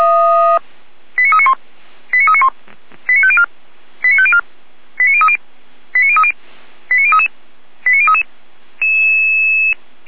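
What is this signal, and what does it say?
Five-tone selective call (ZVEI), heard as quick runs of five tones, tones of about 70 ms. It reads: 89391 89391 89504 89504 89030 89030 89033 89033
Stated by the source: ZVEI five-tone call 89050 received on an SDR.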